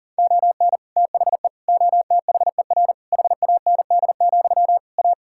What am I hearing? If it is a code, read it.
Morse code, heard as a keyed tone, about 40 wpm, 700 Hz. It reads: ON THE OTHER HAND, A
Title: Sentences from Top 1600 Words 40wpm